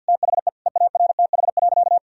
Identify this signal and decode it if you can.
Morse code, heard as a keyed tone, about 50 wpm, 700 Hz. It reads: THE EARTH.
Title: Sentences from Top 1000 Words - Review Part 3 - 50wpm